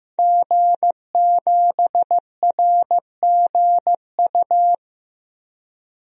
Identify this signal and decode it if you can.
Morse code, heard as a keyed tone, about 15 wpm, 700 Hz. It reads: G7RGU
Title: DXCC-100 Call Signs - 5 Characters - Repeated 3x - 15wpm